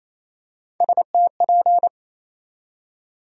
Morse code, heard as a keyed tone, about 28 wpm, 700 Hz. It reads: STP
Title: USA Call Sign Postfixes - 3 Letters 28wpm